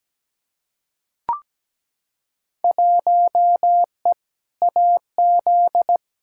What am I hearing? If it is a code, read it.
Morse code, heard as a keyed tone, about 17 wpm, 700 Hz. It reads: 1E AZ
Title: Field Day Exchanges - 17wpm